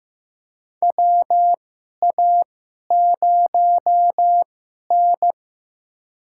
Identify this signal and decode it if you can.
Morse code, heard as a keyed tone, about 15 wpm, spaced as for 11 wpm, 700 Hz. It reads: WA0N
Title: USA Call Signs - 4 Characters - 2x Character Spacing - 15wpm